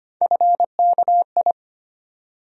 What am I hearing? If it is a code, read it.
Morse code, heard as a keyed tone, about 25 wpm, 700 Hz. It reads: FKI
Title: USA Call Sign Postfixes - 3 Letters 25wpm